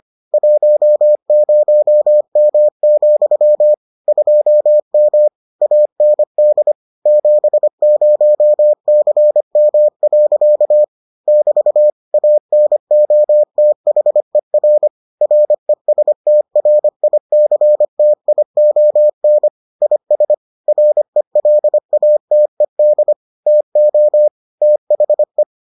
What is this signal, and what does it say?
Morse code, heard as a keyed tone, about 25 wpm, 600 Hz. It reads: 10M, 2M AND 70CM. <BT> ANOTHER RESTRICTION IS RELATED TO THE